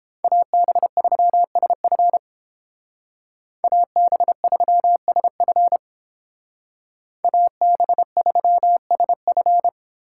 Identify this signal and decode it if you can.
Morse code, heard as a keyed tone, about 30 wpm, 700 Hz. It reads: AB3SF AB3SF AB3SF